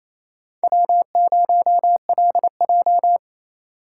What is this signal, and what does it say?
Morse code, heard as a keyed tone, about 36 wpm, 700 Hz. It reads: W0LJ